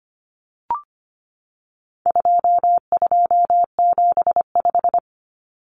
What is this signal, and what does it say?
Morse code, heard as a keyed tone, about 25 wpm, 700 Hz. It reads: 2275
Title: Common Cut Numbers - 4-Digit 25wpm